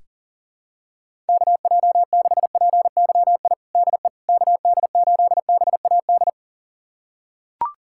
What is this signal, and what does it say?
Morse code, heard as a keyed tone, about 40 wpm, 700 Hz. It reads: KJ6PYI DE KD8BAD
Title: USA Callsign DE Callsign - 6 Characters - 40wpm